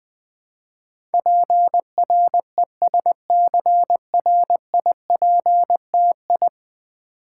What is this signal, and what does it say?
Morse code, heard as a keyed tone, about 20 wpm, 700 Hz. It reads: PRESCRIPTI
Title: Build A Word - 5 Letters At A Time - 12-Letter Words - 20wpm